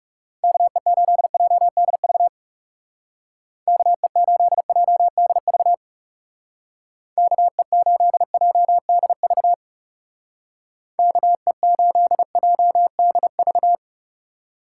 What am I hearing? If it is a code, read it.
Morse code, heard as a keyed tone, about 36 wpm, 700 Hz. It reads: KE8JDV KE8JDV KE8JDV KE8JDV
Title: Speed Racing - USA 2x3 Call Signs 30wpm